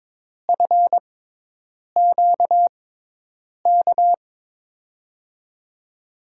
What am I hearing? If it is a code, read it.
Morse code, heard as a keyed tone, about 22 wpm, spaced as for 8 wpm, 700 Hz. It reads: FQK
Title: USA Call Signs - 6x Character Spacing - 22wpm